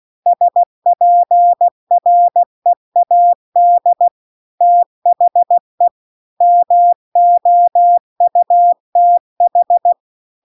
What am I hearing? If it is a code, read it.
Morse code, heard as a keyed tone, about 16 wpm, 700 Hz. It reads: SPREAD THE MOUTH